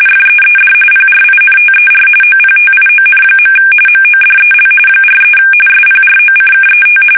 RTTY, about 45.45 baud. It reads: FEED PRINTER OPTION. MODEL 43 SPROCKET
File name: fsk850 / teletype-corp